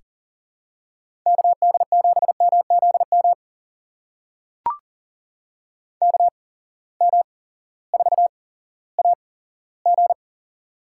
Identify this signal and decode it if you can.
Morse code, heard as a keyed tone, about 38 wpm, spaced as for 18 wpm, 700 Hz. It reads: KD7MZM KM4AG
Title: USA Call Signs - 8x Character Spacing - 40wpm